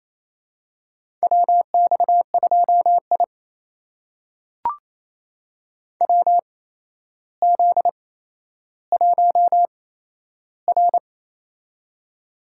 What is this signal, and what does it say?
Morse code, heard as a keyed tone, about 28 wpm, spaced as for 12 wpm, 700 Hz. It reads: WX2I WZ1R
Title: USA Call Signs - 4 Characters - 8x Character Spacing - 28wpm